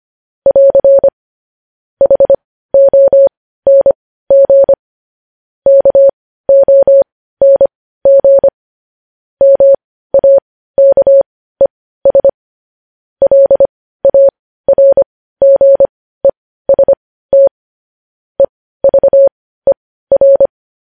Morse code, heard as a keyed tone, about 25 wpm, spaced as for 15 wpm, 550 Hz. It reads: <AR> HONG KONG MAKES LARGEST EVER